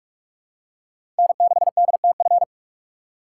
Morse code, heard as a keyed tone, about 44 wpm, 700 Hz. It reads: N6DTF
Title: USA Call Signs - 5 Characters - 3x Character Spacing - 45wpm